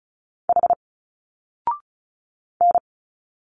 Morse code, heard as a keyed tone, about 34 wpm, 700 Hz. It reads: H N
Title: Single Letters 35wpm